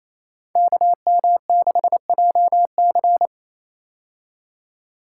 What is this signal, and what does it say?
Morse code, heard as a keyed tone, about 28 wpm, 700 Hz. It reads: KM6JC
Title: Code Groups - 5 Letter:Numbers - 28wpm